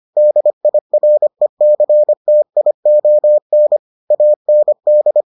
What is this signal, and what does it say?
Morse code, heard as a keyed tone, about 25 wpm, 600 Hz. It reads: DIRECTION AND